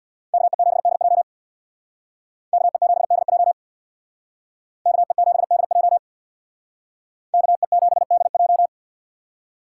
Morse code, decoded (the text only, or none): KE7DJ KE7DJ KE7DJ KE7DJ